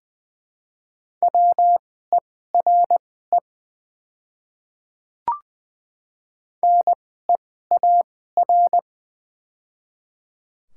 Morse code, heard as a keyed tone, about 20 wpm, spaced as for 15 wpm, 700 Hz. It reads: WERE NEAR
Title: 2x Character Spacing - Top 200 Words - 20wpm